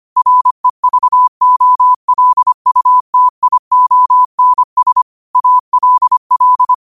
Morse code, heard as a keyed tone, about 25 wpm, 1000 Hz. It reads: REVOLUTIONS ALL
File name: qotdfiles / 25wpm